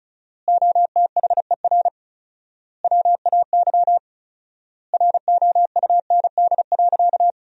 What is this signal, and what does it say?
Morse code, heard as a keyed tone, about 34 wpm, 700 Hz. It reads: OTHER WAY ROUND.
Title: 4x Word-Spacing - Sentences from Top 200 Words - 35wpm